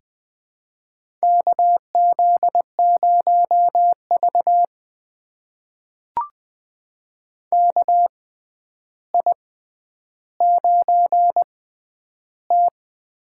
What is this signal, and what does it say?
Morse code, heard as a keyed tone, about 19 wpm, spaced as for 10 wpm, 700 Hz. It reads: KZ0V KI9T
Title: USA Call Signs - 4 Characters - 6x Character Spacing - 20wpm